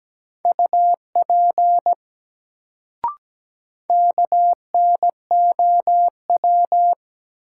Morse code, heard as a keyed tone, about 17 wpm, 700 Hz. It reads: UP KNOW
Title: Sentences from Top 100 Words - New Word Drill - 17wpm